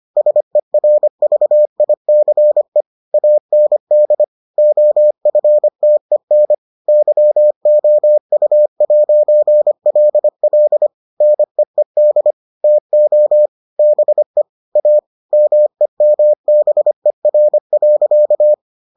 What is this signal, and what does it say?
Morse code, heard as a keyed tone, about 25 wpm, 600 Hz. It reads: SERVICE AND OFTEN YOU'LL NEED TO BE A MEMBER.